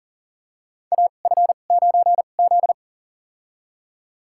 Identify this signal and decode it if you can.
Morse code, heard as a keyed tone, about 40 wpm, spaced as for 29 wpm, 690 Hz. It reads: AF9Z
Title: USA Call Signs - 4 Characters - 2x Character Spacing - 40wpm